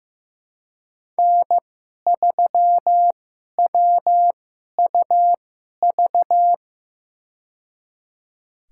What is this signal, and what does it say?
Morse code, heard as a keyed tone, about 15 wpm, spaced as for 11 wpm, 700 Hz. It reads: N3WUV
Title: USA Call Signs - 5 Characters - 2x Character Spacing - 15wpm